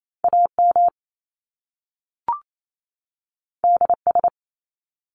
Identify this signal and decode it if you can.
Morse code, heard as a keyed tone, about 28 wpm, 700 Hz. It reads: AM DS
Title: Top Two Letter Combinations in Words 28wpm